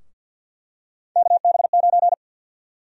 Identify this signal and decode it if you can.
Morse code, heard as a keyed tone, about 51 wpm, 700 Hz. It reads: KB9
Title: USA Call Sign Prefixes - Repeated 2x - 50wpm